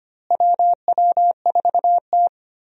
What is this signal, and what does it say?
Morse code, heard as a keyed tone, about 25 wpm, 700 Hz. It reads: WW4T
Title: USA Call Signs - 4 Characters - Repeated 3x - 25wpm